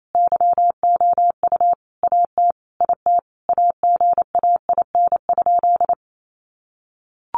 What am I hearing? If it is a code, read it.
Morse code, heard as a keyed tone, about 28 wpm, 700 Hz. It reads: YOU AT IT AGAIN?
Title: Sentences from Top 200 Words 28wpm